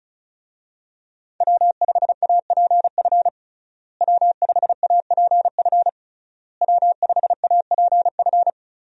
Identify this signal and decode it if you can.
Morse code, heard as a keyed tone, about 36 wpm, 700 Hz. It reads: W5APF W5APF W5APF
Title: USA Call Signs - 5 Characters - Repeated 3x - 35wpm